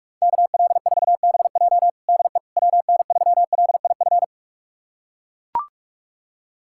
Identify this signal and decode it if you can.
Morse code, heard as a keyed tone, about 42 wpm, 700 Hz. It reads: KL4BJ DE WN3LIF
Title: USA Callsign DE Callsign 45wpm